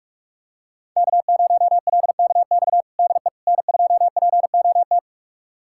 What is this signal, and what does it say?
Morse code, heard as a keyed tone, about 43 wpm, 700 Hz. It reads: K0LKX DE N2POT
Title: USA Callsign DE Callsign - 5 Characters - 45wpm